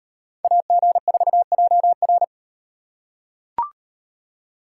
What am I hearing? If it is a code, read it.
Morse code, heard as a keyed tone, about 38 wpm, 700 Hz. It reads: AG4JR